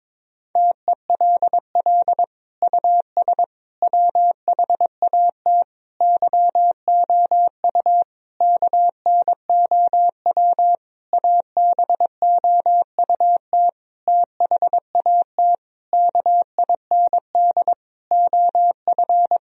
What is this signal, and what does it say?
Morse code, heard as a keyed tone, about 22 wpm, 700 Hz. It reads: TELL US WHAT YOU KNOW ABOUT THAT KIND OF
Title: Sentences from Top 1000 Words - Review Part 4 - 22wpm